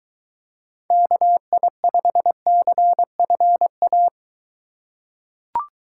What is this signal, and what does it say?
Morse code, heard as a keyed tone, about 21 wpm, 700 Hz. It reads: KI5CFA